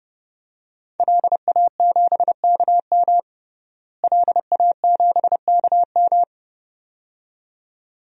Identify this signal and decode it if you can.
Morse code, heard as a keyed tone, about 30 wpm, 700 Hz. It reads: LA7KM LA7KM
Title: DXCC-100 Call Signs - 5 Characters - Repeated 2x - 30wpm